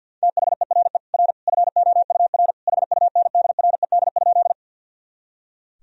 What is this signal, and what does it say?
Morse code, heard as a keyed tone, about 51 wpm, 700 Hz. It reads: THERE R FOUR HUNDRED?
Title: Sentences from Top 400 Words and Abbreviations 50wpm